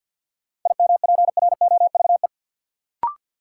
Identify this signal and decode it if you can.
Morse code, heard as a keyed tone, about 53 wpm, 700 Hz. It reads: IMPROVE
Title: Sentences from Top 800 Words - New Word Drill - 50wpm